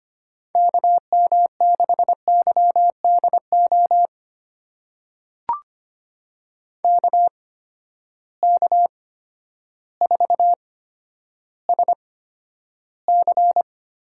Morse code, heard as a keyed tone, about 25 wpm, spaced as for 11 wpm, 700 Hz. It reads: KM6YDO KK4SC
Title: USA Call Signs - 6 Characters - 8x Character Spacing - 25wpm